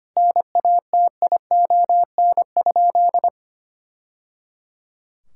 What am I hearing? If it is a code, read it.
Morse code, heard as a keyed tone, about 25 wpm, 700 Hz. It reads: NATION?
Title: Sentences from Top 1000 Words - Review Part 6 - 25wpm